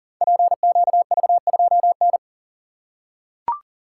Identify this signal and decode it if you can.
Morse code, heard as a keyed tone, about 39 wpm, 690 Hz. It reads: PQV2N
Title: Code Groups - 5 Letter:Numbers - 40wpm